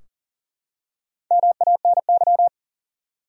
Morse code, heard as a keyed tone, about 38 wpm, 700 Hz. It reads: MANY
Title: Top 100 Words 40wpm